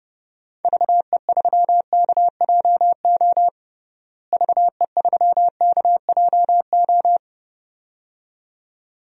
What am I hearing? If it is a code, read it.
Morse code, heard as a keyed tone, about 30 wpm, 700 Hz. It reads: VE3KJO VE3KJO
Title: DXCC-100 Call Signs - 6 Characters - Repeated 2x - 30wpm